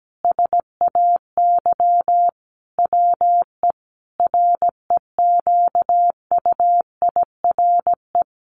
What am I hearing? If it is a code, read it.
Morse code, heard as a keyed tone, about 17 wpm, 700 Hz. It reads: SAY WE REQUIRE
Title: Sentences from Top 1500 Words 17wpm